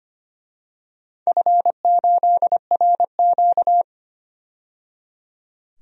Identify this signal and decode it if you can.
Morse code, heard as a keyed tone, about 25 wpm, 700 Hz. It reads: F8RQ
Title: DXCC-100 Call Signs - 4 Characters - 25wpm